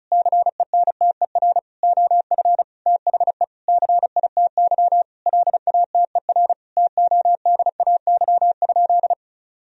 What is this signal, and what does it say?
Morse code, heard as a keyed tone, about 36 wpm, 700 Hz. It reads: CENTER OF THE CITY LATER TODAY?
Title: Sentences from Top 1000 Words - Review Part 2 - 35wpm